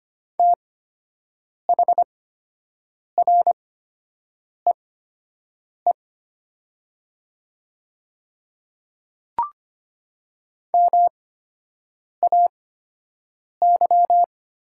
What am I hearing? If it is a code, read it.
Morse code, heard as a keyed tone, about 25 wpm, spaced as for 7 wpm, 700 Hz. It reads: THREE MAY
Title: 8x Character Spacing - Top 500 Words - 25wpm